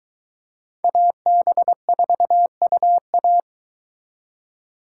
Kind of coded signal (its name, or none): Morse code